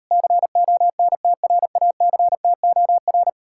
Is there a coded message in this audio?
Yes: Morse code from 0.1 to 3.3 s, heard as a keyed tone, about 40 wpm, 700 Hz. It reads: CONTRACTOR